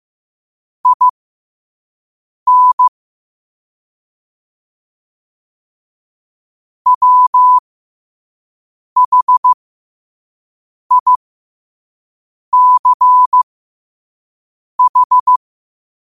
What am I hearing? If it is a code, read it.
Morse code, heard as a keyed tone, about 15 wpm, spaced as for 5 wpm, 1000 Hz. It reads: IN WHICH
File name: qotdfiles / 5wpm